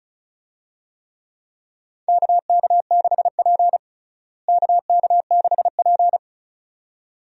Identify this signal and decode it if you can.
Morse code, heard as a keyed tone, about 34 wpm, 700 Hz. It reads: KK6P KK6P
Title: USA Call Signs - 4 Characters - Repeated 2x - 35wpm